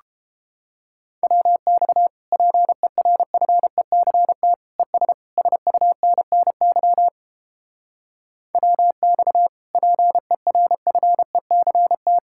Morse code, heard as a keyed tone, about 31 wpm, 700 Hz. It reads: WX PERFECT ES SUNNY WX PERFECT